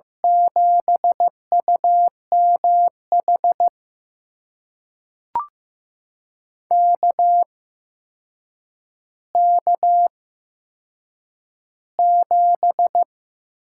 Morse code, heard as a keyed tone, about 15 wpm, spaced as for 7 wpm, 700 Hz. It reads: 7UMH KK7
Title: USA Call Signs - 8x Character Spacing - 15wpm